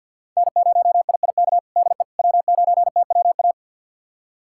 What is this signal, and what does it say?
Morse code, heard as a keyed tone, about 52 wpm, 700 Hz. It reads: N0IIK DE W9TWA
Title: USA Callsign DE Callsign - 5 Characters - 50wpm